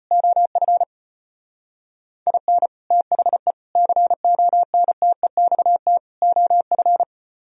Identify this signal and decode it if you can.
Morse code, heard as a keyed tone, about 31 wpm, 700 Hz. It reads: OF IN THE CONTEXT OF